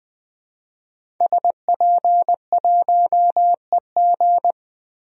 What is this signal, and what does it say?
Morse code, heard as a keyed tone, about 20 wpm, 700 Hz. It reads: SP1EG
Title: DXCC-100 Call Signs - 5 Characters - 20wpm